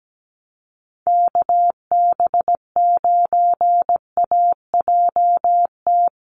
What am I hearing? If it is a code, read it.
Morse code, heard as a keyed tone, about 17 wpm, 700 Hz. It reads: KB9AJT
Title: USA Call Signs Repeated 3x 17wpm